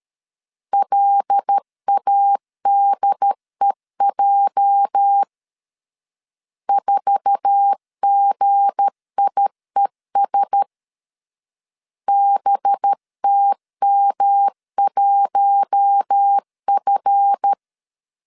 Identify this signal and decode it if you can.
Morse code, heard as a keyed tone, about 13 wpm, 800 Hz. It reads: LADEJ 4GIES BTM1F
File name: LEZIONI / 8 YJ45 / 8 YJ4 12wpm 2151